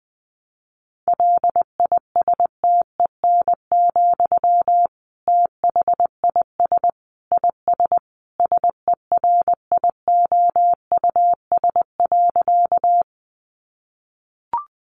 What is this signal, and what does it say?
Morse code, heard as a keyed tone, about 20 wpm, 700 Hz. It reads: LISTEN, THIS IS SERIOUS.
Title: Sentences from Top 1100 Words 20wpm